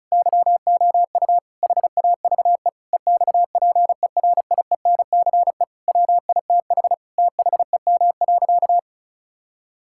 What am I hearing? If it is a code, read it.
Morse code, heard as a keyed tone, about 34 wpm, 700 Hz. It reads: YOU HAVE EXPERIENCE WITH THEM.
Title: Sentences from Top 1000 Words - Review Part 6 - 35wpm